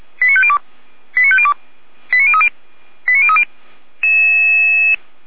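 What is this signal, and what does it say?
Five-tone selective call (ZVEI), heard as quick runs of five tones, tones of about 70 ms. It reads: 89502 89502 89030 89030
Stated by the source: ZVEI five-tone selective call 89391 received by SDR, from the feuerwehr-orlamuende (Orlamünde fire brigade) audio server.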